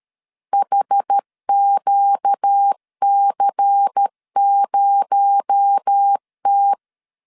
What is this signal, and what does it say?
Morse code, heard as a keyed tone, about 13 wpm, 800 Hz. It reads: HQC0T
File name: CW / DET / 1 9 1500C S A 12wpm 3800